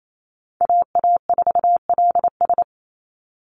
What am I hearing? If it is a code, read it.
Morse code, heard as a keyed tone, about 28 wpm, 700 Hz. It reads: AA4LS
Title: Call Signs Repeated 2x - 28wpm